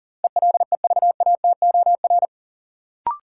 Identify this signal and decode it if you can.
Morse code, heard as a keyed tone, about 40 wpm, 700 Hz. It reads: ELEVATOR
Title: Sentences from Top 1600 Words - New Word Drill - 40wpm